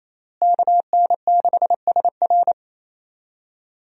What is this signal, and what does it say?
Morse code, heard as a keyed tone, about 28 wpm, 700 Hz. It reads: KN6SR
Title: USA Call Signs - 5 Characters - Repeated 2x - 28wpm